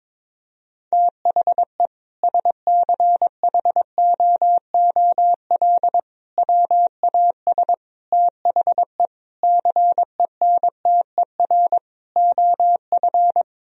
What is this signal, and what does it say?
Morse code, heard as a keyed tone, about 22 wpm, 700 Hz. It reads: THE SCHOOL WAS THE CENTER OF
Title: Sentences from Top 1000 Words - Review Part 2 - 22wpm